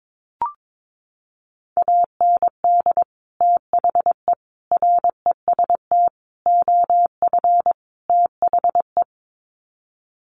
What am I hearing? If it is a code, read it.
Morse code, heard as a keyed tone, about 22 wpm, 700 Hz. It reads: AND THE REST OF THE
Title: Sets of 5 Words 22wpm